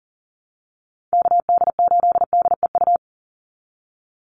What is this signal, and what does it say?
Morse code, heard as a keyed tone, about 40 wpm, 700 Hz. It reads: KD8DEU
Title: NA SOTA Chaser Callsigns 40wpm